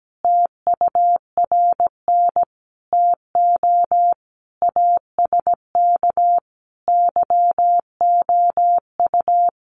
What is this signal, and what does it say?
Morse code, heard as a keyed tone, about 17 wpm, 700 Hz. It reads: TURN TO ASK YOU
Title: Sentences from Top 200 Words 17wpm